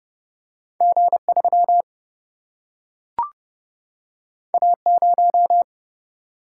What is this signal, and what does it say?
Morse code, heard as a keyed tone, about 30 wpm, 700 Hz. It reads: G3 A0
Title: Single-Letter Single-Number 30wpm